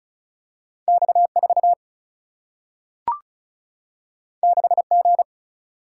Morse code, heard as a keyed tone, about 36 wpm, 700 Hz. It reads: X 4 6 G
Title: Single-Letter Single-Number 35wpm